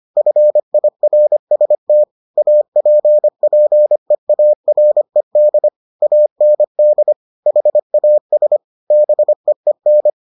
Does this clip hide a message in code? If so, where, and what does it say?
Yes: Morse code from 0.2 to 10.1 s, heard as a keyed tone, about 25 wpm, 600 Hz. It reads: FIRST APPEARED AND HAS BEEN